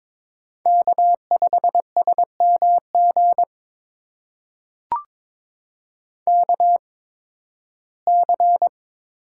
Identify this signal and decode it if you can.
Morse code, heard as a keyed tone, about 22 wpm, spaced as for 14 wpm, 700 Hz. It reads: K5SMG KC
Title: USA Call Signs - 5 Characters - 8x Character Spacing - 22wpm